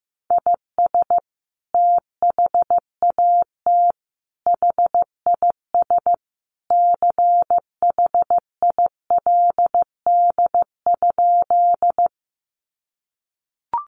Morse code, heard as a keyed tone, about 15 wpm, 700 Hz. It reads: IS THAT HIS CHILD?
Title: Sentences from Top 800 Words 15wpm